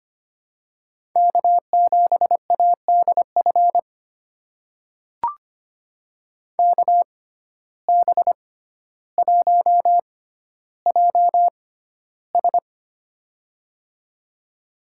Morse code, heard as a keyed tone, about 25 wpm, spaced as for 13 wpm, 700 Hz. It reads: K7ADF KB1JS
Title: USA Call Signs - 5 Characters - 6x Character Spacing - 25wpm